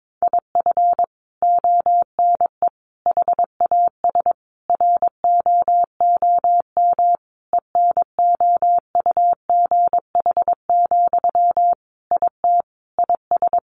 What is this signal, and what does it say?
Morse code, heard as a keyed tone, about 22 wpm, 700 Hz. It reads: IF ONE HAS ROOM ENOUGH, IT IS
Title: Sentences from Top 1400 Words 22wpm